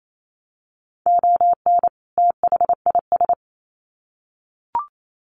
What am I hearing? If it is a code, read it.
Morse code, heard as a keyed tone, about 28 wpm, 700 Hz. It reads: ON THIS
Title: Sets of 2 Words 28wpm